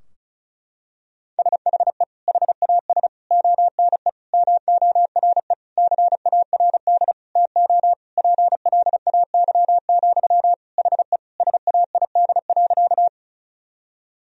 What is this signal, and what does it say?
Morse code, heard as a keyed tone, about 36 wpm, 700 Hz. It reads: SHE HAS ONE MORE CARD TO PLAY, HE SAID.
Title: Sentences from Top 1000 Words - Review Part 4 - 35wpm